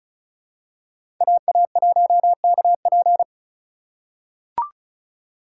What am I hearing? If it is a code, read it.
Morse code, heard as a keyed tone, about 34 wpm, 700 Hz. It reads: AA1KP
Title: USA Call Signs - 5 Characters - 2x Character Spacing - 35wpm